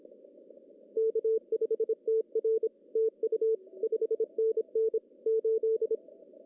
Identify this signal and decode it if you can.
Morse code, heard as a keyed tone, about 26 wpm, 440 Hz. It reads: K5TR TU 5NN 8
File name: CW / cqww-cw-k5tr